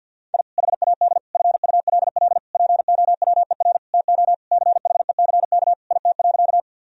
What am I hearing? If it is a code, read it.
Morse code, heard as a keyed tone, about 47 wpm, 700 Hz. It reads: I HAD FULL POWER TO CHECK IT.
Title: Sentences from Top 1000 Words - Review Part 3 - 50wpm